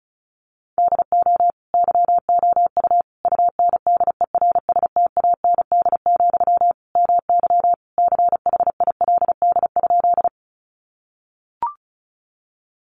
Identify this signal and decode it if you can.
Morse code, heard as a keyed tone, about 34 wpm, 700 Hz. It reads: DO YOU UNDERSTAND, MY CHILD?
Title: Sentences from Top 800 Words 35wpm